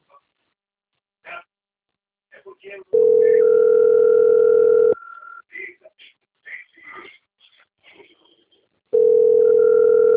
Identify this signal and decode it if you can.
Ringback tone (North America), 440 + 480 Hz, with speech.
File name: rg-1600-unknown- - - .55103